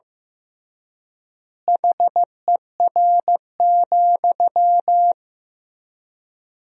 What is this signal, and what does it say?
Morse code, heard as a keyed tone, about 15 wpm, 700 Hz. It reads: HER,
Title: 3x Word-Spacing - Sentences from Top 200 Words - 15wpm